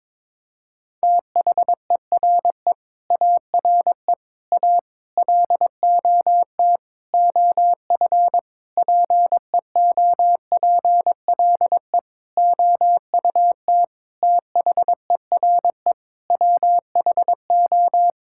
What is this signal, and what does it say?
Morse code, heard as a keyed tone, about 22 wpm, 700 Hz. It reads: THERE ARE A LOT OF PEOPLE OUT THERE WHO
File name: Sets of 9 Words - 22wpm